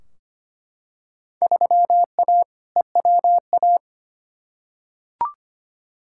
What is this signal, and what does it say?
Morse code, heard as a keyed tone, about 25 wpm, 700 Hz. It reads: 3A EWA
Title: Field Day Exchanges - 25wpm